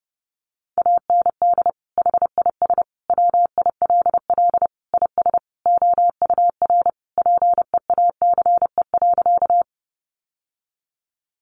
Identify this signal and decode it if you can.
Morse code, heard as a keyed tone, about 30 wpm, 700 Hz. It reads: AND HIS WILL IS OUR PEACE.
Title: Sentences from Top 1500 Words 30wpm